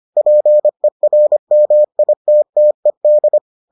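Morse code, heard as a keyed tone, about 25 wpm, 600 Hz. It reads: PERMITTED
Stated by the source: Morse edition of the Foundations of Amateur Radio episodes.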